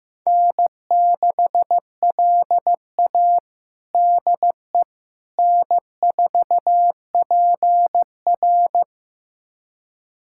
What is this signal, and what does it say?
Morse code, heard as a keyed tone, about 15 wpm, 700 Hz. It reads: N6LA DE N4PR
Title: USA Callsign DE Callsign - 4 Characters - 15wpm